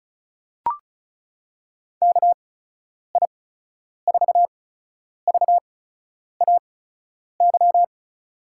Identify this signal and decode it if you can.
Morse code, heard as a keyed tone, about 36 wpm, spaced as for 10 wpm, 700 Hz. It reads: KI4VAY
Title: USA Call Signs - 6 Characters - 8x Character Spacing - 35wpm